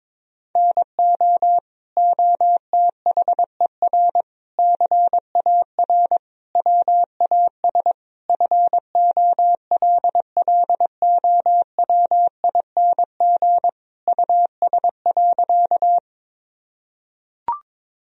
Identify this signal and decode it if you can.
Morse code, heard as a keyed tone, about 21 wpm, 700 Hz. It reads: NO OTHER CAR WAS FOLLOWING US.
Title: Sentences from Top 1000 Words - Review Part 4 - 22wpm